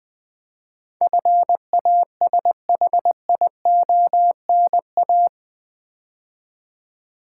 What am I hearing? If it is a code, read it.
Morse code, heard as a keyed tone, about 20 wpm, 700 Hz. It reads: FASHIONA
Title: Build A Word - 4 Letters At A Time - 11-Letter Words - 20wpm